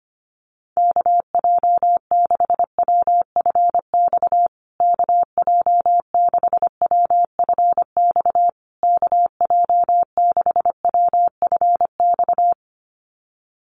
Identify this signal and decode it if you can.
Morse code, heard as a keyed tone, about 25 wpm, 700 Hz. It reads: KJ6WFX KJ6WFX KJ6WFX
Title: USA Call Signs Repeated 3x 25wpm